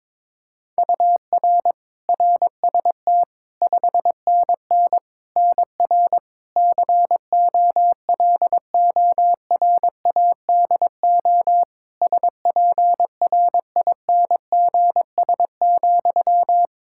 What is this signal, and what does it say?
Morse code, heard as a keyed tone, about 22 wpm, 700 Hz. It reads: UR RST 5NN NR COLORADO SPRINGS,
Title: Practice QSO - Exchange 1 - Beginning - 22wpm